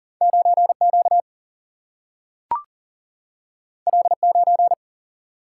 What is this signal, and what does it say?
Morse code, heard as a keyed tone, about 40 wpm, 700 Hz. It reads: 9Q L9
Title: Single-Letter Single-Number 40wpm